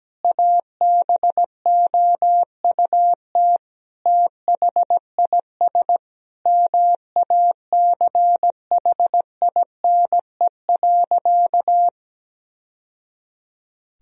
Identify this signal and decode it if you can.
Morse code, heard as a keyed tone, about 17 wpm, 700 Hz. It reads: ABOUT THIS MACHINE.
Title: Sentences from Top 1000 Words - Review Part 2 - 17wpm